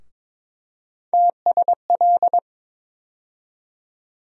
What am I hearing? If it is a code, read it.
Morse code, heard as a keyed tone, about 22 wpm, 700 Hz. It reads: TSL
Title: USA Call Sign Postfixes - 3 Letters 22wpm